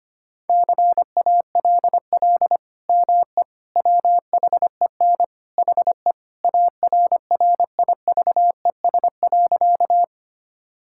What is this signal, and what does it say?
Morse code, heard as a keyed tone, about 25 wpm, 690 Hz. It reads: CALL ME WHEN HE ARRIVES.
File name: practice / 25wpm / Sentences with 2-Word Phrases - 25wpm